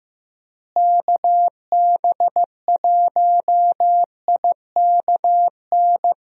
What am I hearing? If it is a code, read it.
Morse code, heard as a keyed tone, about 15 wpm, 700 Hz. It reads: KB1IKN
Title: USA Call Signs - 6 Characters - Repeated 2x - 15wpm